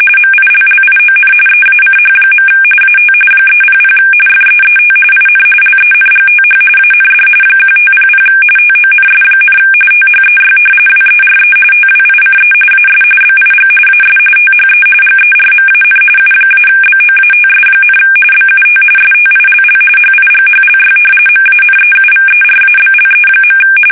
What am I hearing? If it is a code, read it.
RTTY, about 45.45 baud. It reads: MILLION EACH. AT 300 FEET IN LENGTH AND 1500 TONS, THEY WERE TWICE AS BIG AS THE GERMAN U-BOATS, BUT STILL HIGHLY MANEUVERABLE.
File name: fsk850